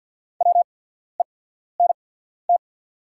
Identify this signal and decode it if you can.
Morse code, heard as a keyed tone, about 48 wpm, spaced as for 14 wpm, 690 Hz. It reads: WENT
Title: 8x Character Spacing - Top 300 Words - 50wpm